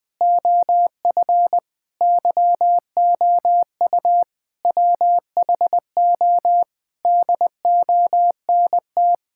Morse code, heard as a keyed tone, about 20 wpm, 700 Hz. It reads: OF YOU WHO DONT
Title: Sentences with 7-Word Phrases - 20wpm